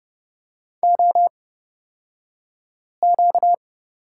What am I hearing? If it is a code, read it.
Morse code, heard as a keyed tone, about 30 wpm, spaced as for 5 wpm, 700 Hz. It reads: OQ
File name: course / 30:10wpm / 235 Lesson 38d - Letter q - Two Characters 30wpm